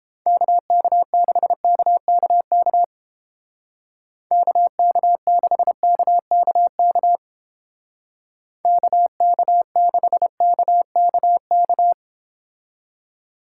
Morse code, heard as a keyed tone, about 30 wpm, 700 Hz. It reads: KK6KKK KK6KKK KK6KKK